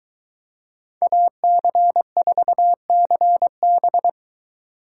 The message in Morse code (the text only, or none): AC4CB